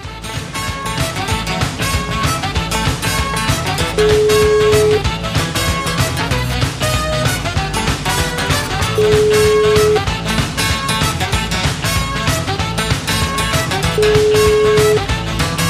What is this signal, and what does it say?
Ringback tone (Europe, 425 Hz), with noise.